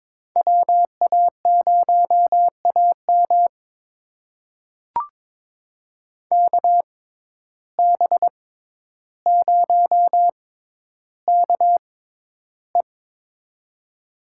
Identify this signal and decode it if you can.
Morse code, heard as a keyed tone, about 22 wpm, spaced as for 11 wpm, 700 Hz. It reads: WA0AM KB0KE
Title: USA Call Signs - 5 Characters - 6x Character Spacing - 22wpm